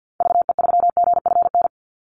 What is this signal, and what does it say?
Morse code, heard as a keyed tone, about 50 wpm, 700 Hz. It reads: VE3GFN
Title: Call Signs Repeated 2x - 50wpm